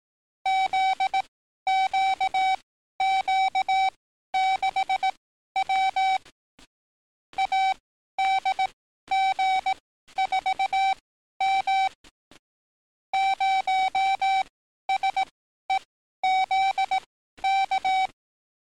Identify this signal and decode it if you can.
Morse code, heard as a keyed tone, about 18 wpm, spaced as for 12 wpm, 770 Hz. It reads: ZQQ6W ADG4M 0SEZK
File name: training / NAVPERS 11049to11060 / NAVPERS 11052 RA RB